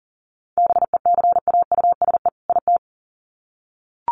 Morse code, heard as a keyed tone, about 40 wpm, 700 Hz. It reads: BECAUSE IT
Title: Sets of 2 Words 40wpm